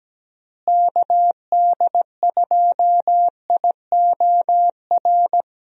Morse code, heard as a keyed tone, about 17 wpm, 700 Hz. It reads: KD2IOR
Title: USA Call Signs - 6 Characters - Repeated 2x - 17wpm